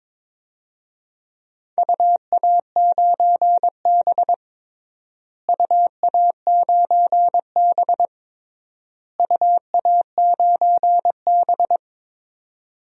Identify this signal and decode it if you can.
Morse code, heard as a keyed tone, about 22 wpm, 700 Hz. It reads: UA9B UA9B UA9B
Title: DXCC-100 Call Signs - 4 Characters - Repeated 3x - 22wpm